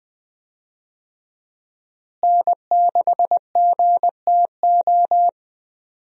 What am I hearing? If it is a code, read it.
Morse code, heard as a keyed tone, about 20 wpm, 700 Hz. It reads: N6GTO
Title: USA Call Signs - 5 Characters - Repeated 3x - 20wpm